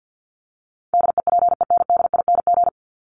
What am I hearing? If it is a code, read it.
Morse code, heard as a keyed tone, about 50 wpm, 700 Hz. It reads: DEPENDING
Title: ING Suffix - Top 100 Words with Base Words 50wpm